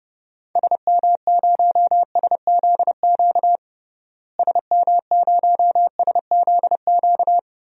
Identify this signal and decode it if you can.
Morse code, heard as a keyed tone, about 32 wpm, 700 Hz. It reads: SM0SZQ SM0SZQ
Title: DXCC-100 Call Signs - 6 Characters - Repeated 3x - 30wpm